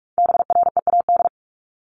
Morse code, heard as a keyed tone, about 44 wpm, 700 Hz. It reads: BREAD
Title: Top 1501-1750 Words 45wpm